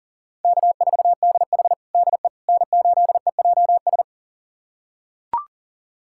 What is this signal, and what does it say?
Morse code, heard as a keyed tone, about 40 wpm, 700 Hz. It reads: K4DH DE N8EJS
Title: USA Callsign DE Callsign 40wpm